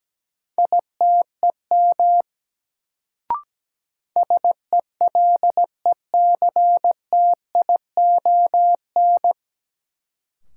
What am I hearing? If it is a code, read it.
Morse code, heard as a keyed tone, about 17 wpm, 700 Hz. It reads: ITEM SELECTION
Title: Sentences from Top 1400 Words - New Word Drill Alt - 17wpm